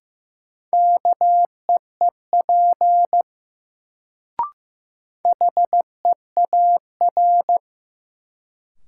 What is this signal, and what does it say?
Morse code, heard as a keyed tone, about 15 wpm, 700 Hz. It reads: KEEP HEAR
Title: Sentences from Top 300 Words - New Word Drill - 15wpm